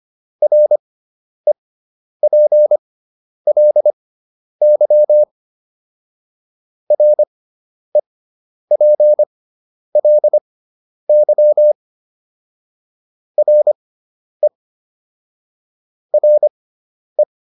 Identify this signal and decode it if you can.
Morse code, heard as a keyed tone, about 25 wpm, spaced as for 10 wpm, 600 Hz. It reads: REPLY REPLY RE RE